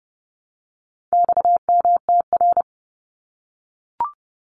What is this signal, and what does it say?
Morse code, heard as a keyed tone, about 30 wpm, 700 Hz. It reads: XMTR
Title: Abbreviations 30wpm